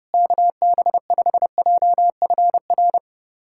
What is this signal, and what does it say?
Morse code, heard as a keyed tone, about 30 wpm, 700 Hz. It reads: KB5JFR
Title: USA Callsign DE Callsign - 6 Characters - 30wpm